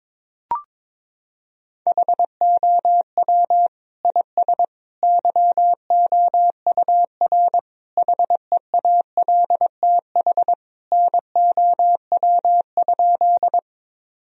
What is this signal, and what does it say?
Morse code, heard as a keyed tone, about 22 wpm, 700 Hz. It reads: HOW IS YOUR HEALTH NOW?
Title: Sentences from Top 1000 Words - Review Part 3 - 22wpm